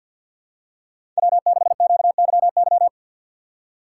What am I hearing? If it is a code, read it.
Morse code, heard as a keyed tone, about 52 wpm, 700 Hz. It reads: W6QYY
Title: USA Call Signs - 5 Characters - 3x Character Spacing - 50wpm